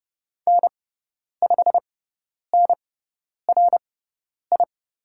Morse code, heard as a keyed tone, about 30 wpm, spaced as for 10 wpm, 700 Hz. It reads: N5NRI
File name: Lesson 16g - Letter c - Call Signs 30wpm